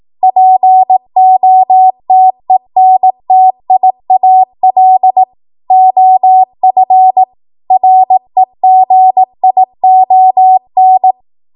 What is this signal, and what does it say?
Morse code, heard as a keyed tone, about 18 wpm, 750 Hz. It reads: POTENTIAL OF REGION
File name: Morse / 18WPM